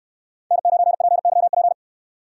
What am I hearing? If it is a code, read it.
Morse code, heard as a keyed tone, about 69 wpm, 700 Hz. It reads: N9PYP